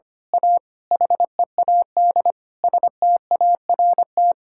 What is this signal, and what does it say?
Morse code, heard as a keyed tone, about 25 wpm, 700 Hz. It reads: A HEAD START
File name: Sets of 9 Words - 25wpm